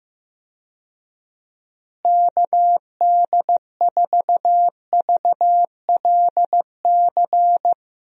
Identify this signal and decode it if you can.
Morse code, heard as a keyed tone, about 15 wpm, 700 Hz. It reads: KD4VLC
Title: USA Call Signs - 6 Characters - Repeated 2x - 15wpm